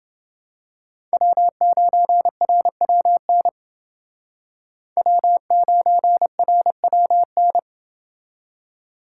Morse code, heard as a keyed tone, about 29 wpm, 700 Hz. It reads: W9RWN W9RWN